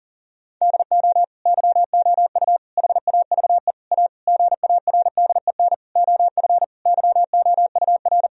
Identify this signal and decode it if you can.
Morse code, heard as a keyed tone, about 40 wpm, 700 Hz. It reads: DO YOU HAVE A GARDEN OF YOUR
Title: Sentences from Top 1000 Words - Review Part 4 - 40wpm